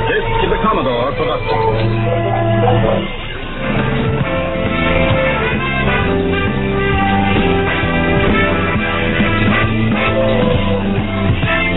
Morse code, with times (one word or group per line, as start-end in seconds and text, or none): none